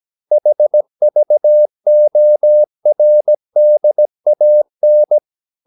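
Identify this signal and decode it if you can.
Morse code, heard as a keyed tone, about 17 wpm, 600 Hz. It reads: HVORDAN
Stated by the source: Slow (QRS) Morse practice at 17 wpm, from CW med Gnister.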